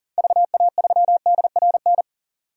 Morse code, heard as a keyed tone, about 40 wpm, 700 Hz. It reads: VA3DRN
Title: DXCC-100 Call Signs - 6 Characters - 40wpm